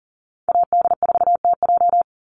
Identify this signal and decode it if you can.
Morse code, heard as a keyed tone, about 40 wpm, 700 Hz. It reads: AD4TJ
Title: Call Signs - 40wpm